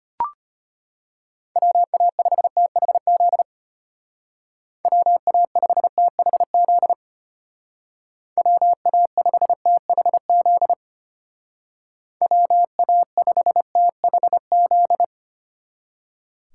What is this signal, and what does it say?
Morse code, heard as a keyed tone, about 32 wpm, 700 Hz. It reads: WA5THZ WA5THZ WA5THZ WA5THZ